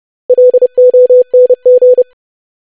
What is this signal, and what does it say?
Morse code, heard as a keyed tone, about 30 wpm, 500 Hz. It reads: LONG
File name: lesson / web / basic500-25wpm